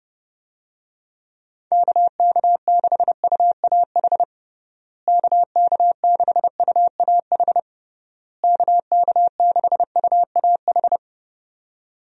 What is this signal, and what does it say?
Morse code, heard as a keyed tone, about 30 wpm, 700 Hz. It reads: KK6UAH KK6UAH KK6UAH
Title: USA Call Signs - 6 Characters - Repeated 3x - 30wpm